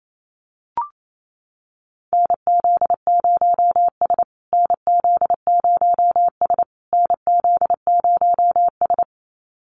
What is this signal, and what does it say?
Morse code, heard as a keyed tone, about 27 wpm, 700 Hz. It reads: NZ0S NZ0S NZ0S
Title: USA Call Signs Repeated 3x 28wpm